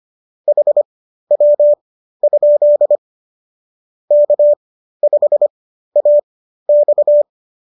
Morse code, heard as a keyed tone, about 25 wpm, spaced as for 13 wpm, 590 Hz. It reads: HW? K5AX